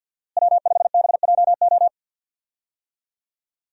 Morse code, heard as a keyed tone, about 51 wpm, 700 Hz. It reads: W5BJO
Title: USA Call Signs - 5 Characters - Repeated 3x - 50wpm